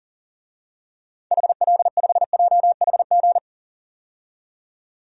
Morse code, heard as a keyed tone, about 40 wpm, 700 Hz. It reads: HL5JHG
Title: DXCC-100 Call Signs - 6 Characters - 40wpm